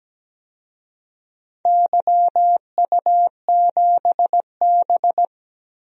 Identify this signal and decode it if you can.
Morse code, heard as a keyed tone, about 17 wpm, 700 Hz. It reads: YU7B
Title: DXCC-100 Call Signs - 4 Characters - Repeated 3x - 17wpm